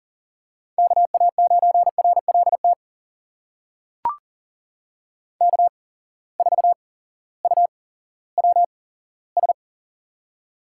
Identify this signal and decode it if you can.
Morse code, heard as a keyed tone, about 40 wpm, spaced as for 18 wpm, 700 Hz. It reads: KA9RLT K4UWS
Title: USA Call Signs - 8x Character Spacing - 40wpm